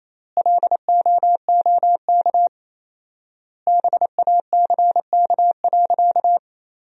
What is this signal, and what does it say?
Morse code, heard as a keyed tone, about 28 wpm, 700 Hz. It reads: LOOK BACK.
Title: 4x Word-Spacing - Sentences from Top 300 Words - 28wpm